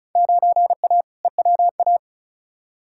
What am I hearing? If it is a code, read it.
Morse code, heard as a keyed tone, about 36 wpm, 690 Hz. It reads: 9A EWA
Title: Field Day Exchanges Encore - 35wpm